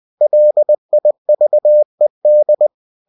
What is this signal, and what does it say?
Morse code, heard as a keyed tone, about 20 wpm, 600 Hz. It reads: LIVED
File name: cw-books / winnie-the-pooh / winnie-20wpm-page-0020